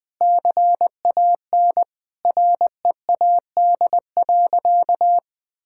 Morse code, heard as a keyed tone, about 21 wpm, 700 Hz. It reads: CAN READ.
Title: Sentences with 8-Word Phrases - Spoken Beginning - 20wpm